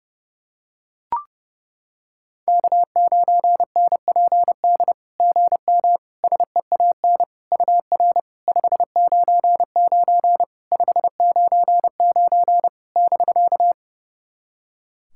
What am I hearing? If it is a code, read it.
Morse code, heard as a keyed tone, about 30 wpm, 690 Hz. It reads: K9NPD GM SEAN UR 599 599 <BK>
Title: SOTA QSO Practice 30wpm